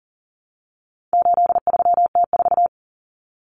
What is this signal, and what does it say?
Morse code, heard as a keyed tone, about 40 wpm, 700 Hz. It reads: 83T4
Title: Common Cut Numbers - 4-Digit 40wpm